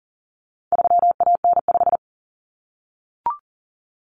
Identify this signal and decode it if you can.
Morse code, heard as a keyed tone, about 40 wpm, 700 Hz. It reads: 3AN5
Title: Common Cut Numbers - 4-Digit 40wpm